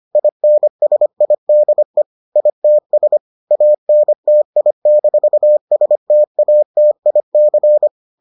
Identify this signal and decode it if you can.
Morse code, heard as a keyed tone, about 25 wpm, 600 Hz. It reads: INSIDE ITS ANTI-STATIC